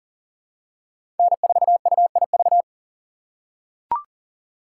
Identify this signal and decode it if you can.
Morse code, heard as a keyed tone, about 40 wpm, 700 Hz. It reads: N4UIV
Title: USA Call Signs - 6x Character Spacing - 40wpm